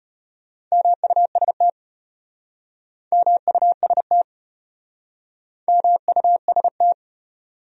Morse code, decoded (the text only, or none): MUST MUST MUST